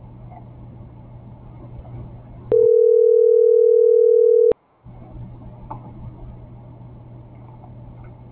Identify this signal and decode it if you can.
Ringback tone (North America), 440 + 480 Hz.